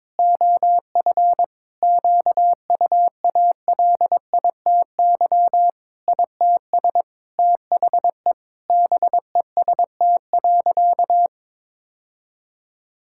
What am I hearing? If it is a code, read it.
Morse code, heard as a keyed tone, about 22 wpm, 700 Hz. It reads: OF QUALITY ITS THE BEST.
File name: practice / 22wpm / Sentences with 3-Word Phrases - 22wpm